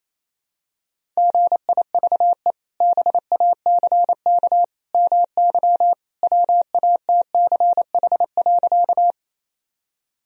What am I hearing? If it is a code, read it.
Morse code, heard as a keyed tone, about 27 wpm, 700 Hz. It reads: GIVE BACK MY WATCH.
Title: Sentences from Top 1000 Words - Review Part 1 - 28wpm